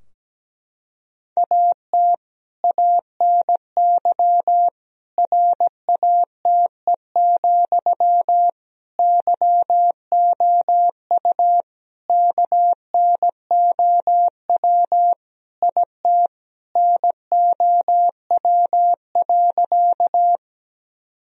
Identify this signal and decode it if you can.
Morse code, heard as a keyed tone, about 17 wpm, 700 Hz. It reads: AT ANY RATE, YOU KNOW IT NOW.
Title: Sentences from Top 1000 Words - Review Part 4 - 17wpm